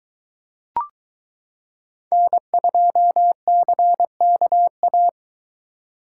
Morse code, heard as a keyed tone, about 22 wpm, 700 Hz. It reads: N2CKA